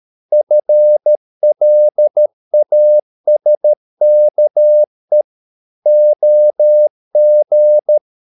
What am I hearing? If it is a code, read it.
Morse code, heard as a keyed tone, about 13 wpm, 600 Hz. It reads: FLASKE OG
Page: Ferie 13 wpm